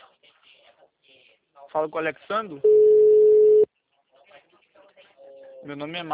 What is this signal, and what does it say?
Ringback tone (Europe, 425 Hz), with speech.